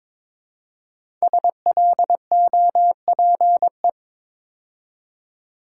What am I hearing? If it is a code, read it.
Morse code, heard as a keyed tone, about 22 wpm, 700 Hz. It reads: SLOPE
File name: 5 Letter Words - First 3 Letters Given - 22wpm